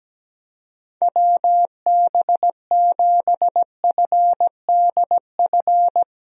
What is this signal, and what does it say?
Morse code, heard as a keyed tone, about 17 wpm, 700 Hz. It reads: WB7FDF
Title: USA Call Signs - 6 Characters - Repeated 2x - 17wpm